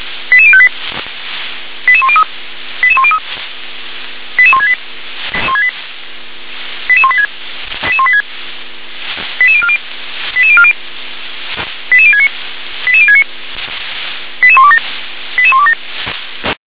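Five-tone selective call (ZVEI), heard as quick runs of five tones, tones of about 70 ms. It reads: 88058 88103 88103 88168 88186 88186 88040 88040 88069 88069 88127 88127
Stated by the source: Five-tone ZVEI selective call received by SDR.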